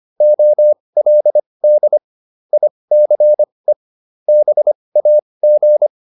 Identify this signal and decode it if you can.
Morse code, heard as a keyed tone, about 25 wpm, spaced as for 20 wpm, 600 Hz. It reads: OLD ICE BAG